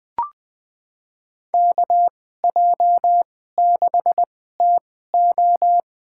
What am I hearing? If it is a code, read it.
Morse code, heard as a keyed tone, about 20 wpm, spaced as for 14 wpm, 700 Hz. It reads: KJ6TO
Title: USA Call Signs - 2x Character Spacing - 20wpm